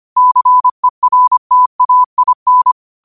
Morse code, heard as a keyed tone, about 25 wpm, 1000 Hz. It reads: CERTAIN
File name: qotdfiles / 25wpm